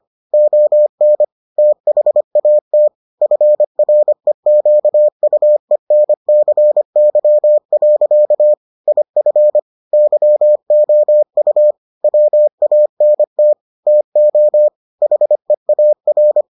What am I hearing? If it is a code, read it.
Morse code, heard as a keyed tone, about 25 wpm, 600 Hz. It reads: ON THAT FREQUENCY. IF YOU WANT TO HEAR